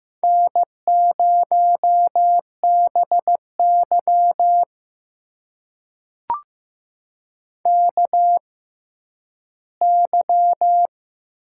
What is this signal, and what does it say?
Morse code, heard as a keyed tone, about 15 wpm, spaced as for 10 wpm, 700 Hz. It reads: N0BY KY
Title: USA Call Signs - 4 Characters - 6x Character Spacing - 15wpm